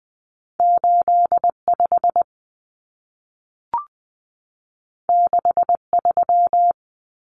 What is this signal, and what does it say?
Morse code, heard as a keyed tone, about 20 wpm, 700 Hz. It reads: 85 63
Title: Common Cut Numbers - 2-Digit 20wpm